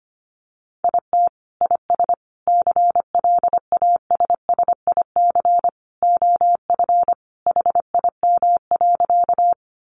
Morse code, heard as a keyed tone, about 25 wpm, 700 Hz. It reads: IT IS CLASSIC OF HIM.
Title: Sentences from Top 1000 Words 25wpm